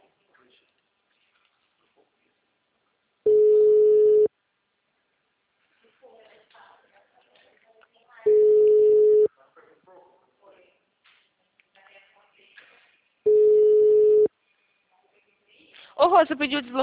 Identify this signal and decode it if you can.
Ringback tone (Europe, 425 Hz), with speech.
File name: out- -2003- - - .81343